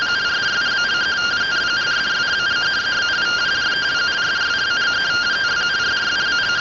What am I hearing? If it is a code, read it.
Two-tone teleprinter signal, about 170 Hz shift.